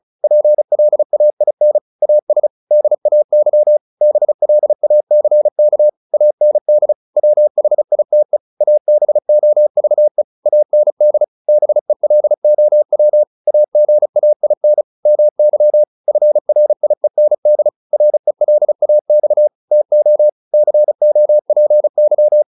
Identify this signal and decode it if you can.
Morse code, heard as a keyed tone, about 34 wpm, 600 Hz. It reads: PLAIN AS DAY BLACK AND WHITE ABOVE AND BELOW AGAIN MY FRIEND RELAX TO COPY